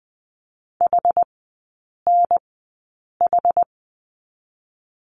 Morse code, heard as a keyed tone, about 20 wpm, spaced as for 8 wpm, 700 Hz. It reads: HNH
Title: 159 Lesson 26e - Letter g - Three Characters 20wpm